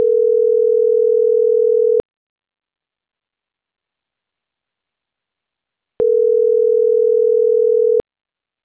Ringback tone (North America), 440 + 480 Hz.